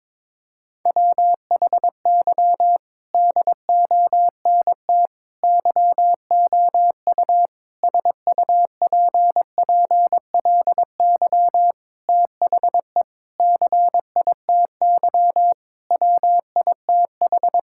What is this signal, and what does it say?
Morse code, heard as a keyed tone, about 22 wpm, 700 Hz. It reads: WHY DONT YOU SUPPLY THE CITY WITH
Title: Sentences from Top 1000 Words - Review Part 5 - 22wpm